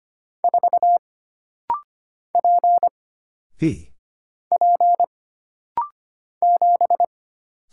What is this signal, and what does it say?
Morse code, heard as a keyed tone, about 25 wpm, 700 Hz. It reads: 4 P P 7
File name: Single Character (Letter:Number:Punctuation:Prosign) - Rapid-Fire - 25wpm